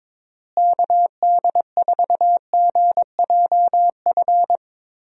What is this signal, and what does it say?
Morse code, heard as a keyed tone, about 22 wpm, 700 Hz. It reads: KD4GJF
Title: USA Call Signs - 6 Characters - Repeated 3x - 22wpm